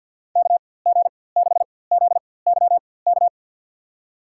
Morse code, heard as a keyed tone, about 52 wpm, spaced as for 23 wpm, 700 Hz. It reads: KG6ZYK
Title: USA Call Signs - 4x Character Spacing - 50wpm